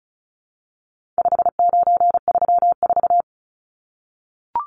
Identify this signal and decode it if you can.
Morse code, heard as a keyed tone, about 34 wpm, 700 Hz. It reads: 5934
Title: Numbers - 4-Digit 35wpm